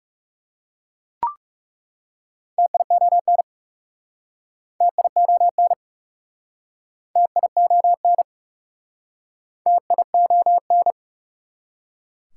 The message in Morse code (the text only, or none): TION TION TION TION